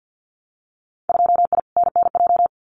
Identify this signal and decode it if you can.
Morse code, heard as a keyed tone, about 50 wpm, 700 Hz. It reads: 2I NNJ
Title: Winter Field Day Exchanges Encore 50wpm